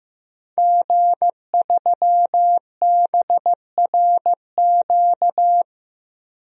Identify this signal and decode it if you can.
Morse code, heard as a keyed tone, about 15 wpm, 700 Hz. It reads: G3BRQ
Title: DXCC-100 Call Signs - 5 Characters - Repeated 2x - 15wpm